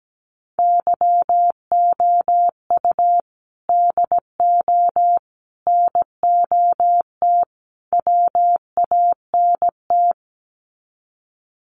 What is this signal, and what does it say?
Morse code, heard as a keyed tone, about 17 wpm, 700 Hz. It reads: YOU DO NOT WANT
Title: Sets of 4 Words 17wpm